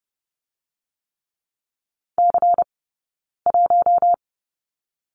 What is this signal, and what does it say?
Morse code, heard as a keyed tone, about 30 wpm, 700 Hz. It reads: C 1
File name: 092 Lesson 16d - Letter c - Two Characters 30wpm